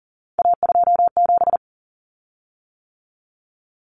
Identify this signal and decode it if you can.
Morse code, heard as a keyed tone, about 40 wpm, 700 Hz. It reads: A27
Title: Common Cut Numbers - 3-Digit 40wpm